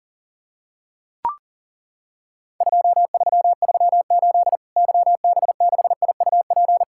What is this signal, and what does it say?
Morse code, heard as a keyed tone, about 41 wpm, 700 Hz. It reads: 2338 YB6IUP
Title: Worked All Europe DX Contest - QTC Exchange - 40wpm